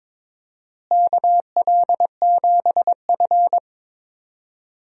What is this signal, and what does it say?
Morse code, heard as a keyed tone, about 22 wpm, 700 Hz. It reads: KL7F
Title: DXCC-100 Call Signs - 4 Characters - Repeated 3x - 22wpm